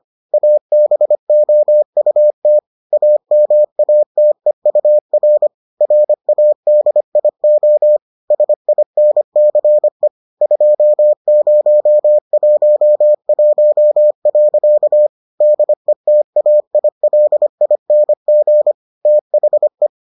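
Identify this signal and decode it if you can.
Morse code, heard as a keyed tone, about 25 wpm, 600 Hz. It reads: ABOUT AMATEUR RADIO SINCE 2011. DETAILING THE
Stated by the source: Morse edition of the Foundations of Amateur Radio episodes.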